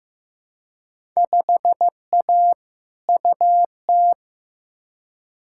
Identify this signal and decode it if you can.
Morse code, heard as a keyed tone, about 15 wpm, 700 Hz. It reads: 5A UT
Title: Field Day Exchanges - 15wpm